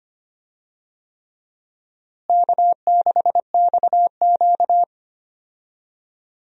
Morse code, heard as a keyed tone, about 25 wpm, 700 Hz. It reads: K6XQ
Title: USA Call Signs - 4 Characters - Repeated 3x - 25wpm